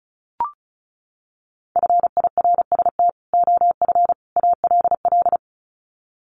Morse code, heard as a keyed tone, about 34 wpm, 700 Hz. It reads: FIRST OF ALL
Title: Sets of 3 Words 35wpm